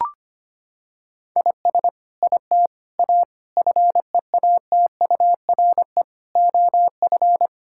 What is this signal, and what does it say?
Morse code, heard as a keyed tone, about 25 wpm, 700 Hz. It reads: IS IT A FEATURE OF
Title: Sentences from Top 1000 Words - Review Part 5 - 25wpm